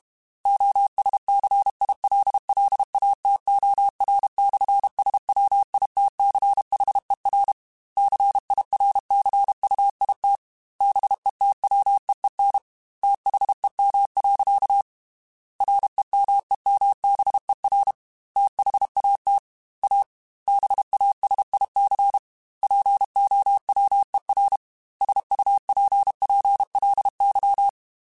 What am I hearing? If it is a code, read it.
Morse code, heard as a keyed tone, about 32 wpm, 800 Hz. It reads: OSCILLATOR/SWITCHER CIRCUIT BETWEEN THEM. REMEMBER THAT A BASIC POWER SUPPLY
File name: Technical Subjects / Technical Subjects-mp3s / Sw Pwr Sup @30